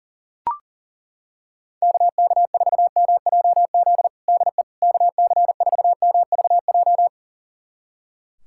Morse code, heard as a keyed tone, about 37 wpm, 700 Hz. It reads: KK4MJZ DE KC4MVJ
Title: USA Callsign DE Callsign - 6 Characters - 40wpm